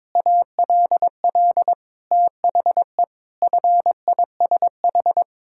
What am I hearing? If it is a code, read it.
Morse code, heard as a keyed tone, about 21 wpm, 700 Hz. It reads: ALL THE FISH
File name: practice / 22wpm / Sets of 8 Words - 22wpm